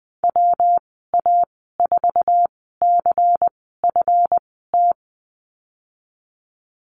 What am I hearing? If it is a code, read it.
Morse code, heard as a keyed tone, about 20 wpm, spaced as for 15 wpm, 700 Hz. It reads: WA4CFT
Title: 200 Lesson 32g - Letter k - Call Signs 20wpm